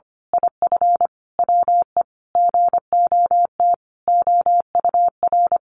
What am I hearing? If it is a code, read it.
Morse code, heard as a keyed tone, about 25 wpm, 700 Hz. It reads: IF WE GOT OUR
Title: A Sound Of Thunder 25wpm